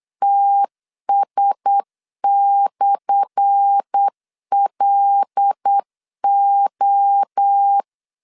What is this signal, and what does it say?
Morse code, heard as a keyed tone, about 8 wpm, 800 Hz. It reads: TS/LO
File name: LEZIONI / 10 SIGNS / 10A SLASH 8wpm 3452